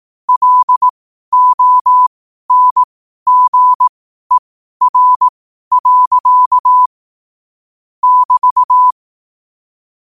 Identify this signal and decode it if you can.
Morse code, heard as a keyed tone, about 18 wpm, spaced as for 12 wpm, 1000 Hz. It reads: LONGER. <BT>
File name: qotdfiles / 12wpm